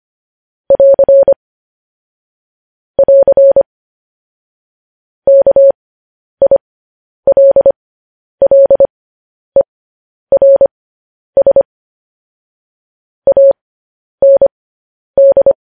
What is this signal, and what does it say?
Morse code, heard as a keyed tone, about 25 wpm, spaced as for 10 wpm, 550 Hz. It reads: <AR> <AR> KILLERS AND
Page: news headlines in morse code at 10 wpm